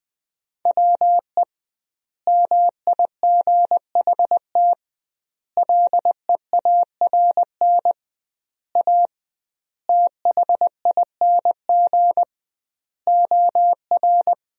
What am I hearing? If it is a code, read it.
Morse code, heard as a keyed tone, about 20 wpm, 700 Hz. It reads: WE MIGHT LEARN A THING OR
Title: 2x Word-Spacing - Sentences from Top 300 Words - 20wpm